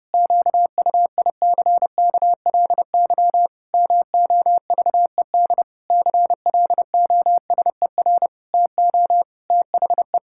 Morse code, heard as a keyed tone, about 30 wpm, 700 Hz. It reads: QUICKLY MOVED CLOSER TO THE
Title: Pangram Challenge v2 - 30wpm